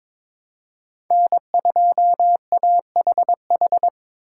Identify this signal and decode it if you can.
Morse code, heard as a keyed tone, about 22 wpm, 700 Hz. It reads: N2AHH
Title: USA Call Signs - 2x Character Spacing - 22wpm